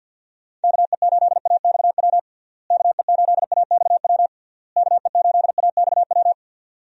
Morse code, heard as a keyed tone, about 47 wpm, 700 Hz. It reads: KE8AXW KE8AXW KE8AXW
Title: USA Call Signs - 6 Characters - Repeated 3x - 50wpm